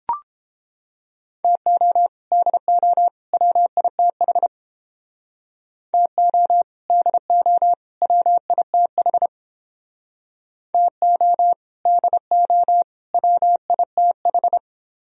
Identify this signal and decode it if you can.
Morse code, heard as a keyed tone, about 29 wpm, 700 Hz. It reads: TO DO WITH TO DO WITH TO DO WITH